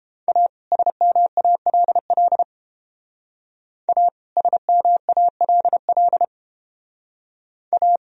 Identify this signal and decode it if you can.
Morse code, heard as a keyed tone, about 30 wpm, 700 Hz. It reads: A SMALL A SMALL A